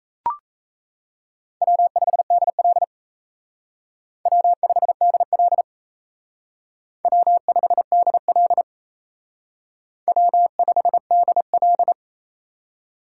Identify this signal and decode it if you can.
Morse code, heard as a keyed tone, about 36 wpm, 700 Hz. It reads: W5DL W5DL W5DL W5DL